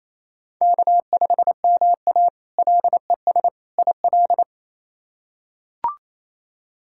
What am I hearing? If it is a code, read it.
Morse code, heard as a keyed tone, about 27 wpm, 700 Hz. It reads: K5MA LES IL
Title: K1USN SST QSO Practice - 28wpm